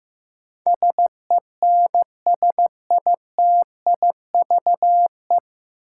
Morse code, heard as a keyed tone, about 15 wpm, 700 Hz. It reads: SENSITIVE
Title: Sentences from Top 1400 Words - New Word Drill Alt - 15wpm